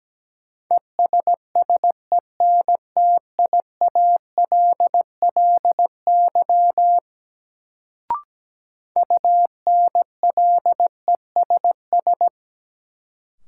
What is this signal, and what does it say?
Morse code, heard as a keyed tone, about 17 wpm, 700 Hz. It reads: ESSENTIALLY UNLESS
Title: Sentences from Top 1300 Words - New Word Drill Alt - 17wpm